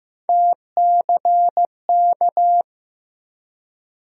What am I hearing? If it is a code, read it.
Morse code, heard as a keyed tone, about 15 wpm, 700 Hz. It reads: TCK
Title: USA Call Sign Postfixes - 3 Letters 15wpm